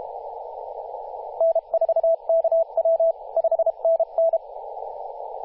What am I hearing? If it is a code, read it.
Morse code, heard as a keyed tone, about 32 wpm, 660 Hz, with radio noise. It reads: N4KW 5NN
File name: posts / 2016-oct-05-2124z-t31t-18-072mhz-cw